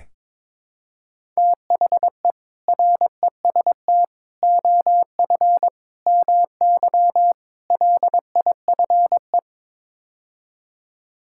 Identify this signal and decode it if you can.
Morse code, heard as a keyed tone, about 22 wpm, 700 Hz. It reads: THE REST OF MY LIFE
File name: practice / 22wpm / Build A Phrase - 5-Word Phrases - 22wpm